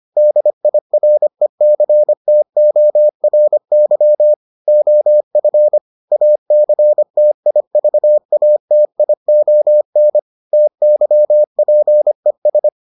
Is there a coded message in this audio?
Yes: Morse code from 0.2 to 12.7 s, heard as a keyed tone, about 25 wpm, 600 Hz. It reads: DIRECTORY OF ACTIVATION TYPES